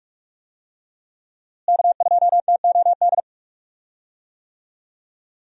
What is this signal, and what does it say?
Morse code, heard as a keyed tone, about 46 wpm, 700 Hz. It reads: K2TOD
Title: USA Call Signs - 5 Characters - Repeated 2x - 45wpm